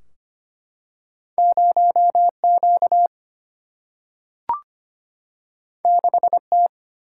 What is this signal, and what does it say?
Morse code, heard as a keyed tone, about 25 wpm, 700 Hz. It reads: 0 Q 6 T
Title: Single-Letter Single-Number 25wpm